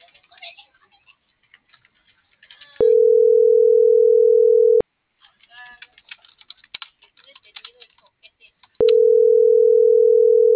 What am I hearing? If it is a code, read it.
Ringback tone (North America), 440 + 480 Hz.